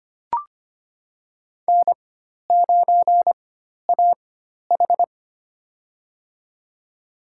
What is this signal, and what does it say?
Morse code, heard as a keyed tone, about 25 wpm, spaced as for 12 wpm, 700 Hz. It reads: N9AH
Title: USA Call Signs - 4 Characters - 4x Character Spacing - 25wpm